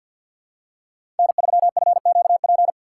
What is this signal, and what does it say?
Morse code, heard as a keyed tone, about 52 wpm, 700 Hz. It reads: N3FQP
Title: USA Call Signs - 5 Characters - 4x Character Spacing - 50wpm